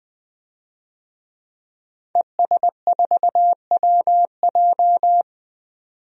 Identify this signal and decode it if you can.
Morse code, heard as a keyed tone, about 20 wpm, 700 Hz. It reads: ES4WJ
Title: DXCC-100 Call Signs - 5 Characters - 20wpm